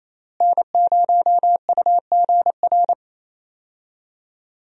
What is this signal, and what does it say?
Morse code, heard as a keyed tone, about 28 wpm, 700 Hz. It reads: N0UGR
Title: USA Call Signs - 5 Characters - Repeated 3x - 28wpm